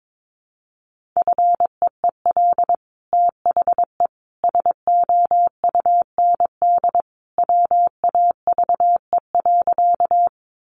Morse code, heard as a keyed tone, about 22 wpm, 700 Hz. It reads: FEEL THE SOUND WAVE.
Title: Sentences from Top 1400 Words 22wpm